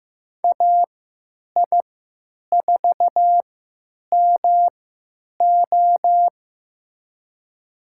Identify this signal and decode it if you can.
Morse code, heard as a keyed tone, about 15 wpm, spaced as for 9 wpm, 700 Hz. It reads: AI4MO
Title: USA Call Signs - 3x Character Spacing - 15wpm